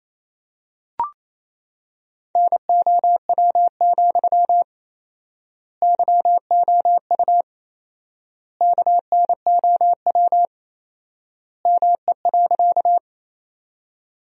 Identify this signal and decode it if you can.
Morse code, heard as a keyed tone, about 28 wpm, 700 Hz. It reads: NOW, YOU KNOW ME.
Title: 4x Word-Spacing - Sentences from Top 200 Words - 28wpm